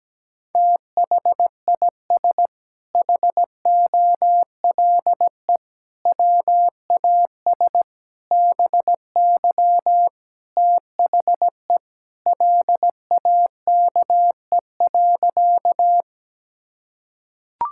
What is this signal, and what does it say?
Morse code, heard as a keyed tone, about 17 wpm, 700 Hz. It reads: THIS HOLE WAS BY THE LAKE.
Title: Sentences from Top 1600 Words 17wpm